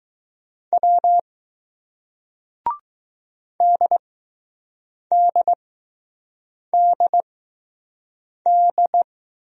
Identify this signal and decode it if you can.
Morse code, heard as a keyed tone, about 20 wpm, spaced as for 6 wpm, 700 Hz. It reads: W DDDD